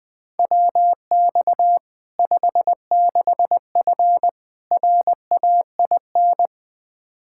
Morse code, heard as a keyed tone, about 20 wpm, 700 Hz. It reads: WX 56F RAIN
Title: Practice QSO - Exchange 2 - Info - 20wpm